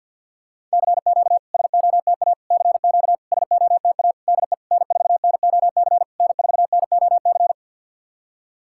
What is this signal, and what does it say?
Morse code, heard as a keyed tone, about 51 wpm, 690 Hz. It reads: CQ SOTA CQ SOTA DE N4NOC N4NOC
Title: SOTA QSO Practice 50wpm